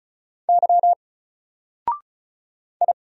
Morse code, heard as a keyed tone, about 34 wpm, 700 Hz. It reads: Y I
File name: Single-Letter-Number-35wpm